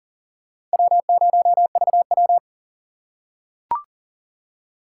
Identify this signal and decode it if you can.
Morse code, heard as a keyed tone, about 40 wpm, 700 Hz. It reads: W0VW
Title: USA Call Signs - 4 Characters - 3x Character Spacing - 40wpm